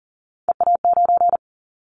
Morse code, heard as a keyed tone, about 40 wpm, 700 Hz. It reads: EA9
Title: Callsign Prefixes for Top 100 DXCC - 40wpm